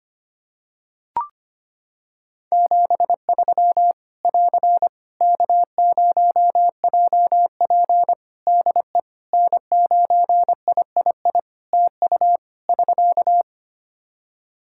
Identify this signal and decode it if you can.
Morse code, heard as a keyed tone, about 25 wpm, 700 Hz. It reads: 73 <AR> K0JP DE N9III TU <SK>
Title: Practice QSO - Exchange 4 - Ending - 25wpm